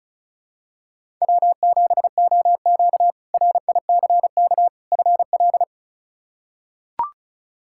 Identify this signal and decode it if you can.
Morse code, heard as a keyed tone, about 36 wpm, 700 Hz. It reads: W7OQ RICK FL
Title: K1USN SST QSO Practice - 35wpm